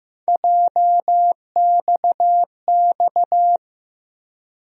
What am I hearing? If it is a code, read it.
Morse code, heard as a keyed tone, about 15 wpm, 700 Hz. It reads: JXX
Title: USA Call Sign Postfixes - 3 Letters 15wpm